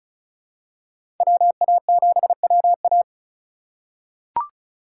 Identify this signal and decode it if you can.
Morse code, heard as a keyed tone, about 34 wpm, 700 Hz. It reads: WA7WA
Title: USA Call Signs - 5 Characters - 2x Character Spacing - 35wpm